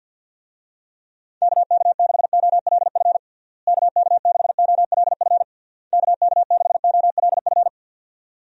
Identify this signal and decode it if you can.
Morse code, heard as a keyed tone, about 46 wpm, 700 Hz. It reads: KK6OLF KK6OLF KK6OLF
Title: USA Call Signs - 6 Characters - Repeated 3x - 50wpm